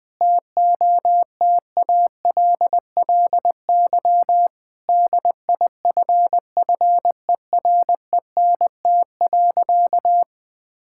Morse code, heard as a keyed tone, about 20 wpm, 700 Hz. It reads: TOTALLY DIFFERENT.
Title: Sentences from Top 1600 Words 20wpm